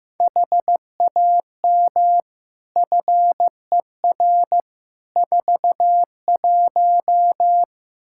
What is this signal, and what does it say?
Morse code, heard as a keyed tone, about 16 wpm, 700 Hz. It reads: HAM FER 41
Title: Practice QSO - Exchange 3 - More Info - 15wpm